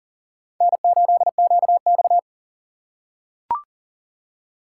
Morse code, heard as a keyed tone, about 38 wpm, 700 Hz. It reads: N8QX
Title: USA Call Signs - 4 Characters - 8x Character Spacing - 40wpm